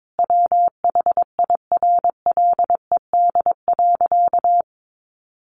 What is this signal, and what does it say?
Morse code, heard as a keyed tone, about 22 wpm, 700 Hz. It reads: WHIRLED.
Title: A Sound Of Thunder 22wpm